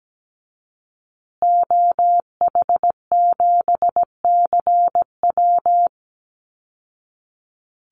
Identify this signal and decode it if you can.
Morse code, heard as a keyed tone, about 17 wpm, 700 Hz. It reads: OH7CW
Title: Call Signs Repeated 3x - 17wpm